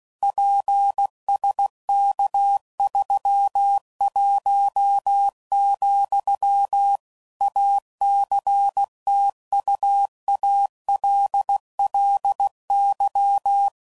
Morse code, heard as a keyed tone, about 16 wpm, 800 Hz. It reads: PSK31, ACTUALLY
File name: CW Code Groups @15WPM / QSOs / kc7sac